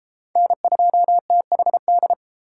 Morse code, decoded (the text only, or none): N2THD